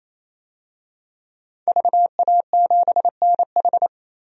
Morse code, heard as a keyed tone, about 28 wpm, 700 Hz. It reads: VA7NH
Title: DXCC-100 Call Signs - 5 Characters - 28wpm